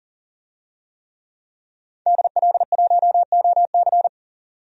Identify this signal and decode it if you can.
Morse code, heard as a keyed tone, about 40 wpm, 700 Hz. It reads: DL1OC
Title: DXCC-100 Call Signs - 5 Characters - 40wpm